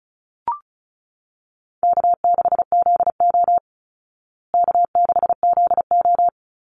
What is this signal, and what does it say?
Morse code, heard as a keyed tone, about 34 wpm, 700 Hz. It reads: K6ZO K6ZO
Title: Easy Call Signs Repeated 3x - 35wpm